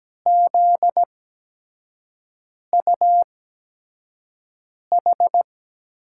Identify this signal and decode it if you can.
Morse code, heard as a keyed tone, about 17 wpm, spaced as for 5 wpm, 700 Hz. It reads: ZUH
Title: USA Call Signs - 6 Characters - 8x Character Spacing - 17wpm